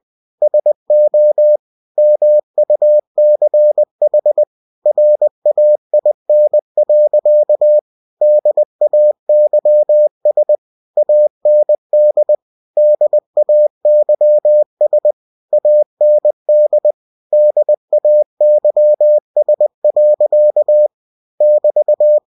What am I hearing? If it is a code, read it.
Morse code, heard as a keyed tone, about 20 wpm, 600 Hz. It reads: SO MUCH RAIN. DAYS AND DAYS AND DAYS. <BT>
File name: cw-books / winnie-the-pooh / winnie-20wpm-page-0090